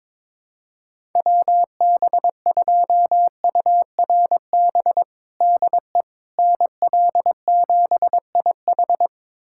Morse code, heard as a keyed tone, about 22 wpm, 700 Hz. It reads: WB2URB DE NL7IH
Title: USA Callsign DE Callsign 22wpm